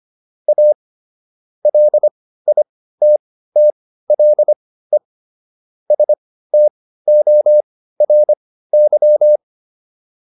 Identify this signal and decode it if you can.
Morse code, heard as a keyed tone, about 25 wpm, spaced as for 15 wpm, 600 Hz. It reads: A LITTLE STORY